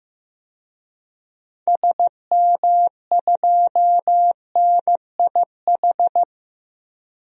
Morse code, heard as a keyed tone, about 15 wpm, 700 Hz. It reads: SM2NIH
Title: DXCC-100 Call Signs - 6 Characters - 15wpm